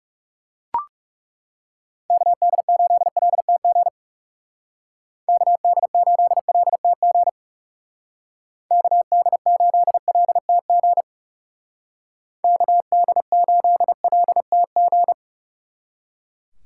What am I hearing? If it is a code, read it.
Morse code, heard as a keyed tone, about 37 wpm, 700 Hz. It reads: KD8LTG KD8LTG KD8LTG KD8LTG
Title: Speed Racing - USA 2x3 Call Signs 30wpm